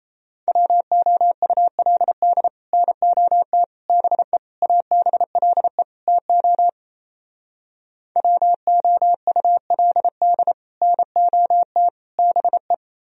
Morse code, heard as a keyed tone, about 37 wpm, 700 Hz. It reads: WOULD NOT BE ABLE TO WOULD NOT BE